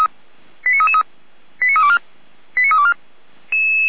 Five-tone selective call (ZVEI), heard as quick runs of five tones, tones of about 80 ms. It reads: 89333 89324 89324